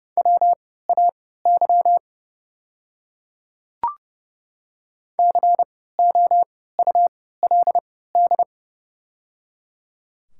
Morse code, heard as a keyed tone, about 31 wpm, spaced as for 17 wpm, 700 Hz. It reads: WAY COULD
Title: 3x Character Spacing - Top 100 Words - 30wpm